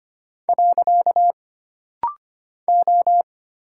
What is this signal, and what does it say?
Morse code, heard as a keyed tone, about 25 wpm, 700 Hz. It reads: . O
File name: Single Character (Letter:Number:Punctuation:Prosign) - Rapid-Fire - 25wpm